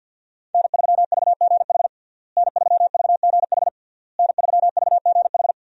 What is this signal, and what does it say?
Morse code, heard as a keyed tone, about 52 wpm, 700 Hz. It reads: N3VGH N3VGH N3VGH
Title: USA Call Signs - 5 Characters - Repeated 3x - 50wpm